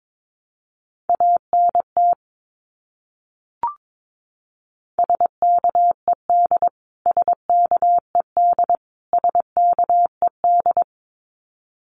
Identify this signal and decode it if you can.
Morse code, heard as a keyed tone, about 22 wpm, 700 Hz. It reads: ANT SKED SKED SKED
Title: Abbreviations Repeated 3x - 22wpm